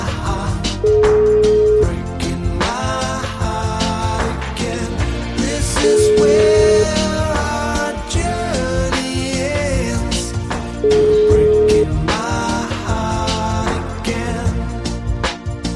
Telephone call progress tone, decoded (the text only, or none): ringback tone